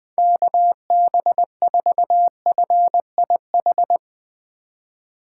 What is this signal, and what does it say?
Morse code, heard as a keyed tone, about 20 wpm, 700 Hz. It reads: KB4FIH
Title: Speed Racing - USA 2x3 Call Signs 20wpm